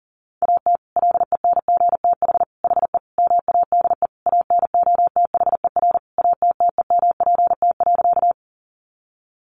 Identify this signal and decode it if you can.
Morse code, heard as a keyed tone, about 40 wpm, 700 Hz. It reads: AT LENGTH HE MADE ANOTHER ATTEMPT.
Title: Sentences from Top 1000 Words 40wpm